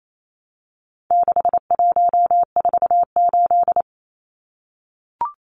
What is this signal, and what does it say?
Morse code, heard as a keyed tone, about 28 wpm, 700 Hz. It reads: 6148
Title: Numbers - 4-Digit 28wpm